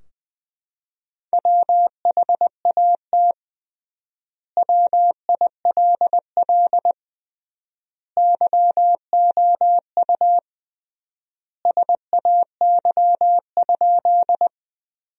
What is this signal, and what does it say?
Morse code, heard as a keyed tone, about 21 wpm, 700 Hz. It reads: WHAT WILL YOU SAY?
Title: 3x Word-Spacing - Sentences from Top 200 Words - 20wpm